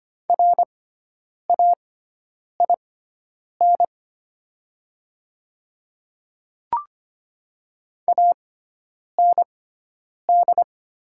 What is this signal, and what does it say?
Morse code, heard as a keyed tone, about 25 wpm, spaced as for 9 wpm, 700 Hz. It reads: RAIN AND
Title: 6x Character Spacing - Top 300 Words - 25wpm